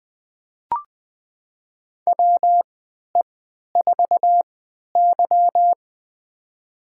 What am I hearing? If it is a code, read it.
Morse code, heard as a keyed tone, about 21 wpm, spaced as for 11 wpm, 700 Hz. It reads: WE4Y
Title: USA Call Signs - 3x Character Spacing - 20wpm